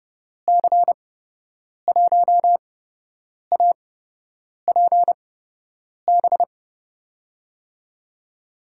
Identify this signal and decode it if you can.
Morse code, heard as a keyed tone, about 30 wpm, spaced as for 8 wpm, 700 Hz. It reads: C1APB
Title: USA Call Signs - 6 Characters - 8x Character Spacing - 30wpm